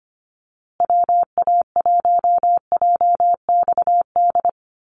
Morse code, heard as a keyed tone, about 25 wpm, 700 Hz. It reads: WA1JXD
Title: USA Call Signs Repeated 2x 25wpm